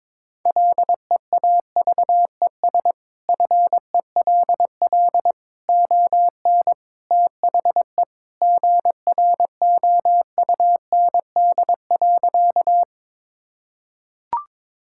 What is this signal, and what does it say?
Morse code, heard as a keyed tone, about 22 wpm, 700 Hz. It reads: LEAVES FELL ON THE GROUND.
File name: practice / 22wpm / Sentences with 3-Word Phrases - 22wpm